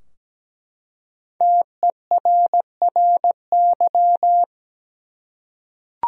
Morse code, heard as a keyed tone, about 17 wpm, 700 Hz. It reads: TERRY
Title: USA Common Names - Max 5 Letters - 17wpm